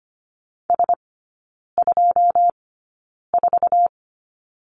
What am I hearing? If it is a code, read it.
Morse code, heard as a keyed tone, about 25 wpm, spaced as for 9 wpm, 700 Hz. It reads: S24
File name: Lesson 14e - Number 2 - Three Characters 25wpm